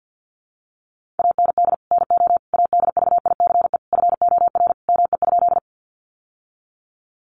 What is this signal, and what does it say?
Morse code, heard as a keyed tone, about 50 wpm, 700 Hz. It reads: AND NO ADVICE FOR ME?
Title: Sentences from Top 1100 Words 50wpm